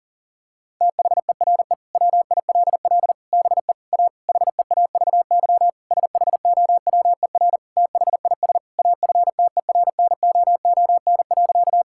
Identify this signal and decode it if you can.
Morse code, heard as a keyed tone, about 40 wpm, 700 Hz. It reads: THERE WILL BE A HEAVY SHOWER THIS AFTERNOON.
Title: Sentences from Top 1600 Words 40wpm